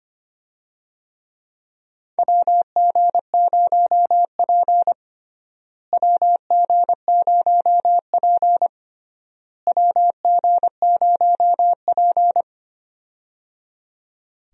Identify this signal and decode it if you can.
Morse code, heard as a keyed tone, about 25 wpm, 700 Hz. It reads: WG0P WG0P WG0P
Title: USA Call Signs - 4 Characters - Repeated 3x - 25wpm